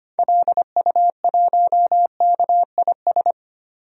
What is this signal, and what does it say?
Morse code, heard as a keyed tone, about 25 wpm, 700 Hz. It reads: LU1KIS
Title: DXCC-100 Call Signs - 6 Characters - Repeated 2x - 25wpm